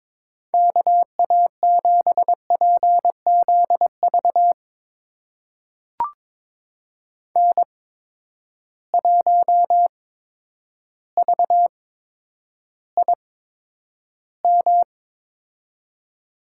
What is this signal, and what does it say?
Morse code, heard as a keyed tone, about 22 wpm, spaced as for 10 wpm, 700 Hz. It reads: KA7PZV N1VIM
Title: USA Call Signs - 8x Character Spacing - 22wpm